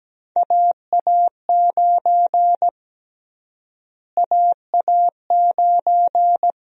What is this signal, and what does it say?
Morse code, heard as a keyed tone, about 17 wpm, 700 Hz. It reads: AA9 AA9
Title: USA Call Sign Prefixes - Repeated 2x - 17wpm